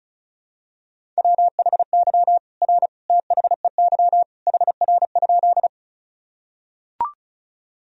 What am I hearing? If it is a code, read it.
Morse code, heard as a keyed tone, about 36 wpm, 700 Hz. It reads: WHY R THEY HR?
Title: Sentences from Top 200 Words and Abbreviations 35wpm